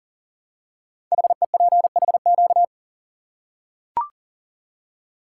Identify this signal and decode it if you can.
Morse code, heard as a keyed tone, about 38 wpm, 690 Hz. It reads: HEPHQ
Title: Code Groups - 5 Letters - 40wpm